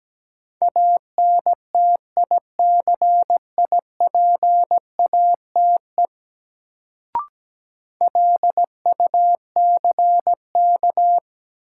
Morse code, heard as a keyed tone, about 17 wpm, 700 Hz. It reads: ANTICIPATE LUCK
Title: Sentences from Top 1500 Words - New Word Drill Alt - 17wpm